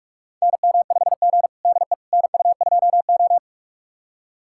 Morse code, heard as a keyed tone, about 41 wpm, 700 Hz. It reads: NM5G DE NU2O
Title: USA Callsign DE Callsign - 4 Characters - 45wpm